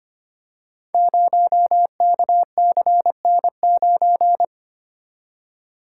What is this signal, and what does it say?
Morse code, heard as a keyed tone, about 25 wpm, 700 Hz. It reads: 0KCN9
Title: Code Groups - 5 Letter:Numbers - 25wpm